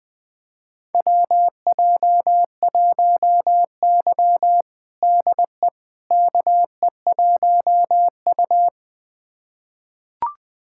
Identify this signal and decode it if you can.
Morse code, heard as a keyed tone, about 20 wpm, 700 Hz. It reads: WJ1Y DE KE1U
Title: USA Callsign DE Callsign - 4 Characters - 20wpm